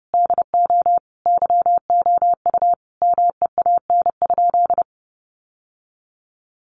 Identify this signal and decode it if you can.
Morse code, heard as a keyed tone, about 30 wpm, 700 Hz. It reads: DO YOU MEAN?
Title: Sentences from Top 900 Words 30wpm